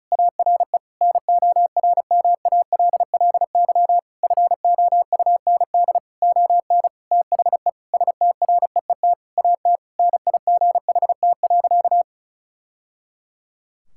Morse code, heard as a keyed tone, about 36 wpm, 700 Hz. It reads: ARE NORMALLY FOUND ON THE STREET AT NIGHT.
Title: Sentences from Top 1000 Words - Review Part 6 - 35wpm